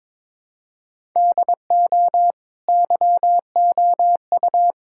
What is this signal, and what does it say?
Morse code, heard as a keyed tone, about 22 wpm, 700 Hz. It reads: DO YOU
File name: practice / 22wpm / Sentences with 2-Word Phrases - 22wpm